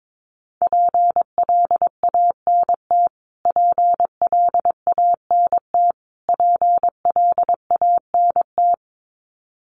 Morse code, heard as a keyed tone, about 22 wpm, 700 Hz. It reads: PLANT PLANT PLANT
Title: Top 500 Words Repeated 22wpm